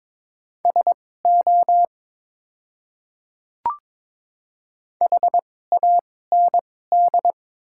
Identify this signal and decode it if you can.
Morse code, heard as a keyed tone, about 22 wpm, spaced as for 16 wpm, 700 Hz. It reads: SO HAND
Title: 2x Character Spacing - Top 200 Words - 22wpm